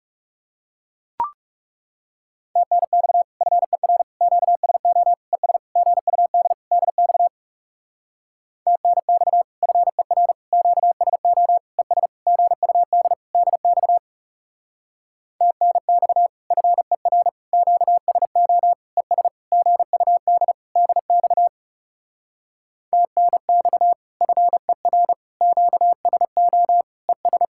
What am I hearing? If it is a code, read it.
Morse code, heard as a keyed tone, about 37 wpm, 700 Hz. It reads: TNX FER QSO ES GUD DX TNX FER QSO ES GUD DX TNX FER QSO ES GUD DX TNX FER QSO ES
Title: Speed-Racing Common QSO Phrases - 30wpm